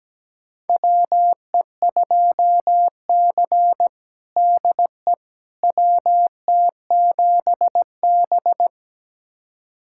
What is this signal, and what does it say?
Morse code, heard as a keyed tone, about 17 wpm, 700 Hz. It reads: WE2C DE WT7B
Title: USA Callsign DE Callsign - 4 Characters - 17wpm